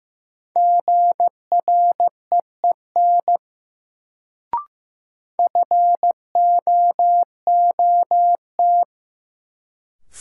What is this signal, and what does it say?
Morse code, heard as a keyed tone, about 15 wpm, 700 Hz. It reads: GREEN FOOT
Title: Sentences from Top 500 Words - New Word Drill - 15wpm